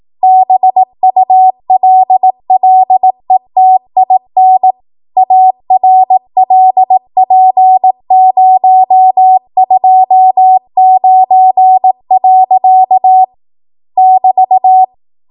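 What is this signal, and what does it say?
Morse code, heard as a keyed tone, about 18 wpm, 750 Hz. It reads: BULLETIN ARLP029. <BT>
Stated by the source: ARRL 18 wpm Morse code practice archive transmission.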